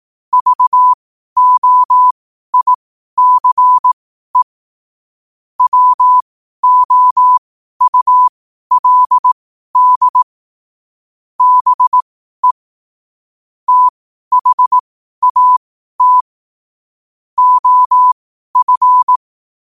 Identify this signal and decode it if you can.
Morse code, heard as a keyed tone, about 18 wpm, spaced as for 12 wpm, 1000 Hz. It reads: VOICE WOULD BE THAT OF
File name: qotdfiles / 12wpm